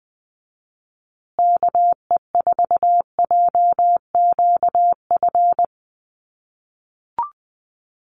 Morse code, heard as a keyed tone, about 20 wpm, 700 Hz. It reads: KE4JQF
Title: USA Call Signs Repeated 3x 20wpm